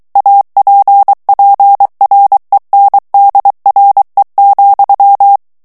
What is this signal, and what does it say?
Morse code, heard as a keyed tone, about 23 wpm, 800 Hz. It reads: APPRENDRE,
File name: morse mp3 / FRANCAIS TEXTE CLAIR 2 VITESSE 22 MOTS MINUTES